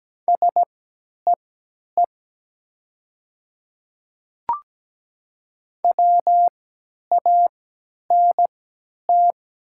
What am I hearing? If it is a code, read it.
Morse code, heard as a keyed tone, about 17 wpm, spaced as for 10 wpm, 700 Hz. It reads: SEE WANT
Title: 3x Character Spacing - Top 200 Words - 17wpm